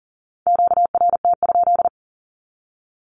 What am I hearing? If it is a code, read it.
Morse code, heard as a keyed tone, about 40 wpm, 700 Hz. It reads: QRT?
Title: Q-Codes Repeated 3x 40wpm